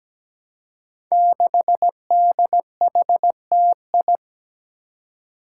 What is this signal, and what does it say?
Morse code, heard as a keyed tone, about 17 wpm, 700 Hz. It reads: 6DHTI
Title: Five Challenging Letter-Number Combinations 17wpm